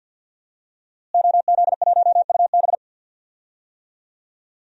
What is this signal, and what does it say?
Morse code, heard as a keyed tone, about 51 wpm, 700 Hz. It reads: OZ1UB
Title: DXCC-100 Call Signs - 5 Characters - Repeated 2x - 50wpm